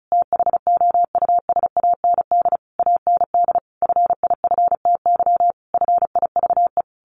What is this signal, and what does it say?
Morse code, heard as a keyed tone, about 34 wpm, 700 Hz. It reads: THOUSAND AND FIFTY FIVE
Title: A Sound Of Thunder 35wpm